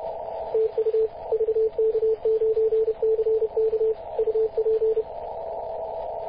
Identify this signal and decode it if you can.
Morse code, heard as a keyed tone, about 31 wpm, 440 Hz, with radio noise. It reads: TU VK9CK UP
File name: posts / 2016-mar-19-0241z-vk9ck-21-032mhz-cw